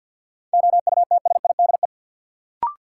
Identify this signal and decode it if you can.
Morse code, heard as a keyed tone, about 52 wpm, 700 Hz. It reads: OUTSIDE
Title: Sentences from Top 900 Words - New Word Drill - 50wpm